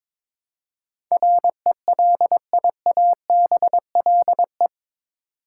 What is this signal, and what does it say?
Morse code, heard as a keyed tone, about 22 wpm, 700 Hz. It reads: RELIABLE
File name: practice / 22wpm / Eight Letter Words - 22wpm